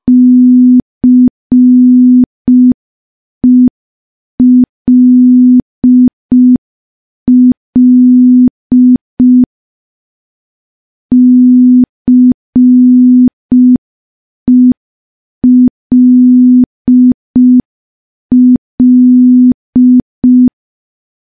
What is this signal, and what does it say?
Morse code, heard as a keyed tone, about 5 wpm, 250 Hz. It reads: CELL CELL
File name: morse / morse-training-05WPM